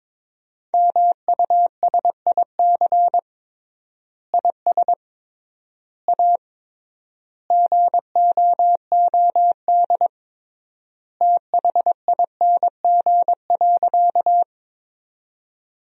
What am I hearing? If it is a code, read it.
Morse code, heard as a keyed tone, about 22 wpm, 700 Hz. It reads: MUSIC IS A GOOD THING.
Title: 3x Word-Spacing - Sentences from Top 300 Words - 22wpm